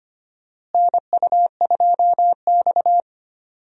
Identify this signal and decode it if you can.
Morse code, heard as a keyed tone, about 25 wpm, 700 Hz. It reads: NU2X
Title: USA Call Signs - 4 Characters - Repeated 3x - 25wpm